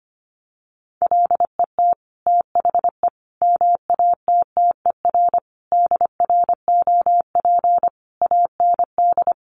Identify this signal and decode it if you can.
Morse code, heard as a keyed tone, about 25 wpm, 700 Hz. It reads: LET THE MATTER DROP AND
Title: Sentences from Top 1400 Words 25wpm